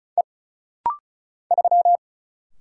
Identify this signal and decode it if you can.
Morse code, heard as a keyed tone, about 36 wpm, 700 Hz. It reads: E 3
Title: Single Letter-Number Alt - Rapid-Fire 35wpm